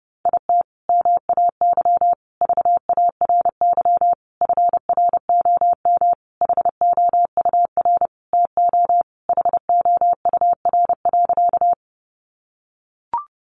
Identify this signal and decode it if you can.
Morse code, heard as a keyed tone, about 28 wpm, 700 Hz. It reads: IT MAY VARY FROM HOUR TO HOUR.
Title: Sentences from Top 1200 Words 30wpm